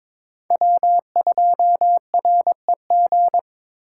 Morse code, heard as a keyed tone, about 22 wpm, 700 Hz. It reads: W2REG
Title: USA Call Signs - 5 Characters - 2x Character Spacing - 22wpm